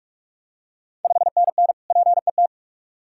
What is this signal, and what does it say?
Morse code, heard as a keyed tone, about 42 wpm, 700 Hz. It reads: 5NN PET
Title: Ontario QSO Party Exchange 45wpm